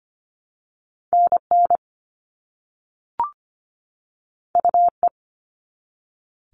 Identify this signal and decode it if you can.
Morse code, heard as a keyed tone, about 25 wpm, 700 Hz. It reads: NN UE
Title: All Cut Numbers - 2-Digit 25wpm